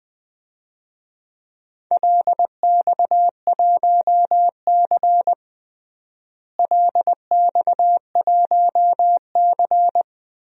Morse code, heard as a keyed tone, about 20 wpm, 700 Hz. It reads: LX1C LX1C
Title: DXCC-100 Call Signs - 4 Characters - Repeated 3x - 20wpm